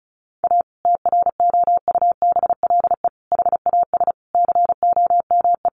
Morse code, heard as a keyed tone, about 34 wpm, 700 Hz. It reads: A TROUBLE HAS COME
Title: Sentences from Top 1200 Words 35wpm